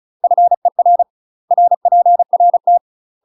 Morse code, heard as a keyed tone, about 34 wpm, 700 Hz. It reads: FER RPRT
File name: UK MASTER QSOs TRAINING / 35WPM / UK MASTER QSOs 0024